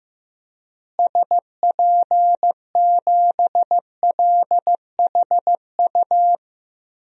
Morse code, heard as a keyed tone, about 15 wpm, 700 Hz. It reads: SP7LHU
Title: DXCC-100 Call Signs - 6 Characters - Repeated 3x - 15wpm